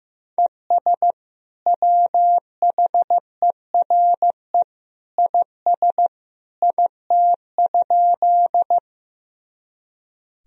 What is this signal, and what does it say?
Morse code, heard as a keyed tone, about 15 wpm, 700 Hz. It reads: ES WHERE IS IT?
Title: Sentences from Top 200 Words and Abbreviations 15wpm